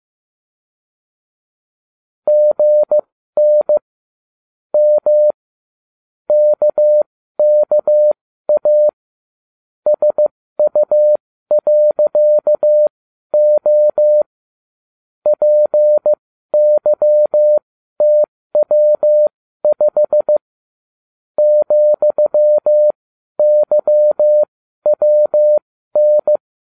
Morse code, heard as a keyed tone, about 15 wpm, spaced as for 12 wpm, 600 Hz. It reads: GN M KKA SU.O PYTW5 ,YWN